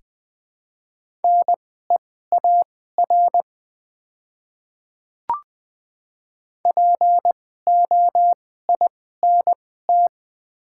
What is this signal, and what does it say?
Morse code, heard as a keyed tone, about 21 wpm, spaced as for 15 wpm, 690 Hz. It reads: NEAR POINT
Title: 2x Character Spacing - Top 200 Words - 20wpm